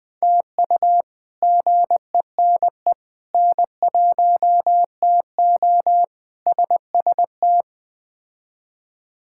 Morse code, heard as a keyed tone, about 20 wpm, 700 Hz. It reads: TU GENE N1TO SST
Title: K1USN SST QSO Practice - 20wpm